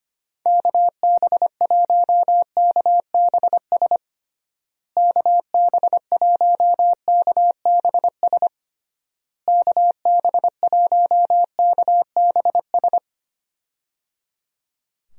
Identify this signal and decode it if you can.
Morse code, heard as a keyed tone, about 25 wpm, 700 Hz. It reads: KB1KBS KB1KBS KB1KBS
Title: USA Call Signs - 6 Characters - Repeated 3x - 25wpm